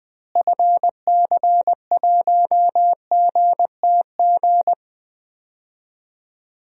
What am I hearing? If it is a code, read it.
Morse code, heard as a keyed tone, about 20 wpm, 700 Hz. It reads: FC1GTG
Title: DXCC-100 Call Signs - 6 Characters - Repeated 3x - 20wpm